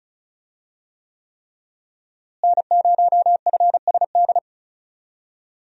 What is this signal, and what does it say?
Morse code, heard as a keyed tone, about 36 wpm, 700 Hz. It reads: N0FSD
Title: USA Call Signs - 5 Characters - Repeated 3x - 35wpm